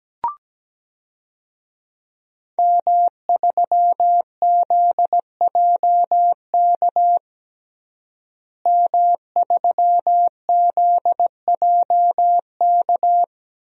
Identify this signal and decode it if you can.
Morse code, heard as a keyed tone, about 17 wpm, 700 Hz. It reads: M3ZJK M3ZJK
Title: DXCC-100 Call Signs - 5 Characters - Repeated 2x - 17wpm